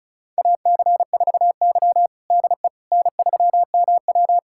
Morse code, heard as a keyed tone, about 36 wpm, 700 Hz. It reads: AC4Y DE N3MW
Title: USA Callsign DE Callsign - 4 Characters - 35wpm